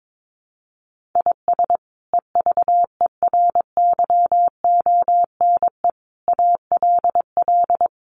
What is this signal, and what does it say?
Morse code, heard as a keyed tone, about 22 wpm, 700 Hz. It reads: IS EVERYONE ALL
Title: Sentences from Top 800 Words 22wpm